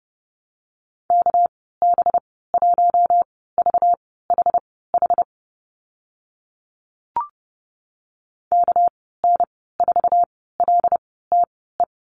Morse code, heard as a keyed tone, about 30 wpm, spaced as for 17 wpm, 700 Hz. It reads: KB1VHH KN4LTE
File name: Lesson 37g - Letter x - Call Signs 30wpm